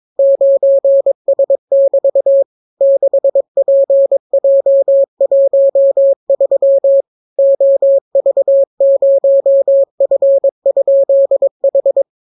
Morse code, heard as a keyed tone, about 22 wpm, 550 Hz. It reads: 9S<BT> 6PJ13 OV0F?H